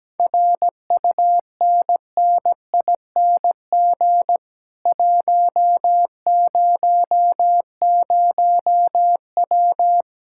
Morse code, heard as a keyed tone, about 17 wpm, 700 Hz. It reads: RUNNING 100W
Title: Speed-Racing Common QSO Phrases - 17wpm